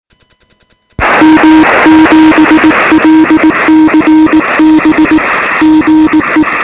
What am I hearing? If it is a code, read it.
Morse code, heard as a keyed tone, about 19 wpm, 340 Hz, with noise. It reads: M7LCB GE
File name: sm7lcb / 2046z ok2bfh 13cm